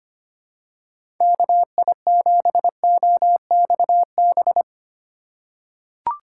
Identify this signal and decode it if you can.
Morse code, heard as a keyed tone, about 25 wpm, 700 Hz. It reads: KI7OXB
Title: USA Call Signs - 6 Characters - 2x Character Spacing - 25wpm